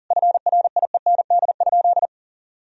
Morse code, heard as a keyed tone, about 37 wpm, 700 Hz. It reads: FRIEND?
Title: Sentences from Top 1000 Words - Review Part 3 - 40wpm